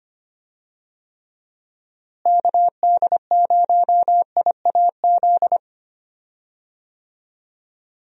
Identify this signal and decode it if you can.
Morse code, heard as a keyed tone, about 25 wpm, 700 Hz. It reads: KD0IAZ
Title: USA Call Signs - 6 Characters - Repeated 2x - 25wpm